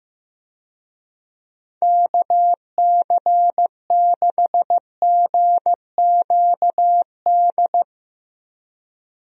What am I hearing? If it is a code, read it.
Morse code, heard as a keyed tone, about 15 wpm, 700 Hz. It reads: KC6GQD
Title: USA Call Signs - 6 Characters - Repeated 2x - 15wpm